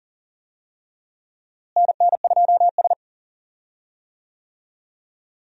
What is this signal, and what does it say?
Morse code, heard as a keyed tone, about 40 wpm, 700 Hz. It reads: NN2S
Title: USA Call Signs - 4 Characters - Repeated 3x - 40wpm